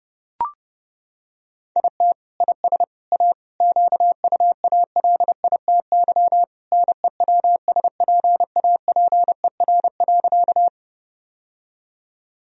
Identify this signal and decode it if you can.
Morse code, heard as a keyed tone, about 30 wpm, 700 Hz. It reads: IT IS A QUALITY NEWSPAPER.
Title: Sentences from Top 1000 Words - Review Part 4 - 30wpm